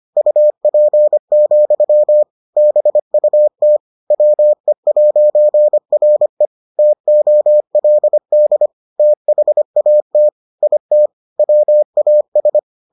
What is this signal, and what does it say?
Morse code, heard as a keyed tone, about 25 wpm, 600 Hz. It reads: UP, BUT WE'RE TOLD THAT IT WAS